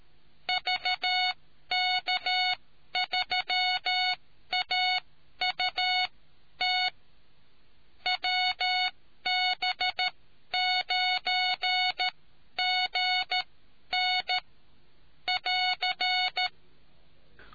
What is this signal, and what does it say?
Morse code, heard as a keyed tone, about 13 wpm, spaced as for 11 wpm, 2200 Hz. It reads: VK3AUT WB9GN <AR>